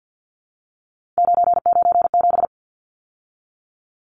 Morse code, heard as a keyed tone, about 50 wpm, 700 Hz. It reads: 997
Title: Numbers - 3-Digit 50wpm